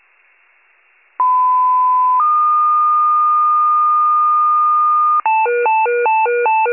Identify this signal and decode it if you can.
Two-tone page, heard as a one-second tone, then a three-second tone, about 1010 then 1220 Hz. It is heard from 1.2 to 5.2 s.